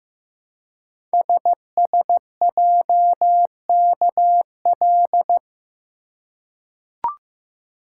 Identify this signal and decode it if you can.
Morse code, heard as a keyed tone, about 15 wpm, 700 Hz. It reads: SSJKL
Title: Code Groups - 5 Letters - 15wpm